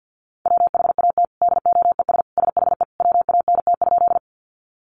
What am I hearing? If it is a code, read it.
Morse code, heard as a keyed tone, about 50 wpm, 700 Hz. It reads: WHAT DOES SHE WANT?
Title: Sentences from Top 200 Words 50wpm